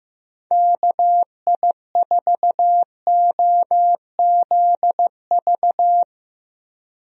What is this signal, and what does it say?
Morse code, heard as a keyed tone, about 15 wpm, 700 Hz. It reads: KI4OZV
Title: USA Call Signs - 6 Characters - Repeated 3x - 15wpm